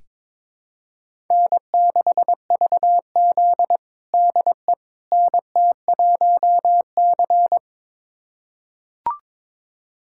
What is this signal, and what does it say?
Morse code, heard as a keyed tone, about 22 wpm, 700 Hz. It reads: N6VZ DE NT1C
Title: USA Callsign DE Callsign - 4 Characters - 22wpm